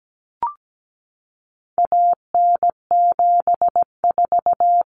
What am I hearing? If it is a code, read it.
Morse code, heard as a keyed tone, about 17 wpm, 700 Hz. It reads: AN74
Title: Common Cut Numbers - 4-Digit 17wpm